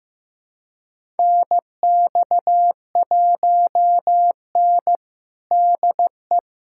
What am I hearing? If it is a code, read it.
Morse code, heard as a keyed tone, about 15 wpm, 700 Hz. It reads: NX1N DE
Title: USA Callsign DE Callsign - 4 Characters - 15wpm